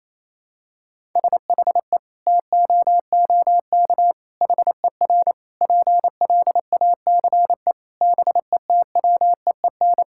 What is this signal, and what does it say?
Morse code, heard as a keyed tone, about 28 wpm, 700 Hz. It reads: SHE TOOK HER PLACE BETWEEN
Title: Sentences from Top 1000 Words - Review Part 5 - 28wpm